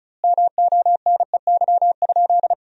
Morse code, heard as a keyed tone, about 34 wpm, 700 Hz. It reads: MONEY?
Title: 2x Word-Spacing - Sentences from Top 400 Words - 35wpm